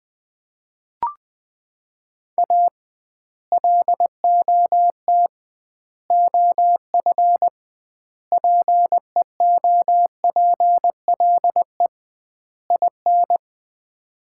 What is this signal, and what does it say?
Morse code, heard as a keyed tone, about 21 wpm, 700 Hz. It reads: A LOT OF PEOPLE IN
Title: 2x Word-Spacing - Sets of 5 Words - 20wpm